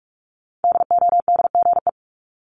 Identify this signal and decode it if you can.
Morse code, heard as a keyed tone, about 46 wpm, 700 Hz. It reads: DODGE
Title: Multi-letter Phonemes Repeated 3x 45wpm